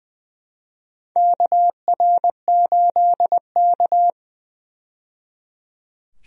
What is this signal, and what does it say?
Morse code, heard as a keyed tone, about 20 wpm, 700 Hz. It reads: KR8K
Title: USA Call Signs - 4 Characters - 20wpm